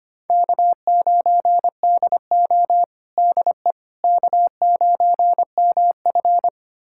Morse code, heard as a keyed tone, about 25 wpm, 700 Hz. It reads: K9DO DE K9MF
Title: USA Callsign DE Callsign - 4 Characters - 25wpm